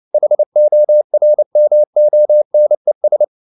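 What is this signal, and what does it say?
Morse code, heard as a keyed tone, about 29 wpm, 600 Hz. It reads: HORMONES
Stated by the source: Morse code at 29 wpm, sending Ken Follett's novel Never in English, from CW med Gnister.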